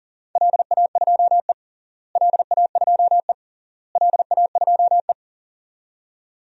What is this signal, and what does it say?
Morse code, heard as a keyed tone, about 40 wpm, 700 Hz. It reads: LA2E LA2E LA2E
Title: DXCC-100 Call Signs - 4 Characters - Repeated 3x - 40wpm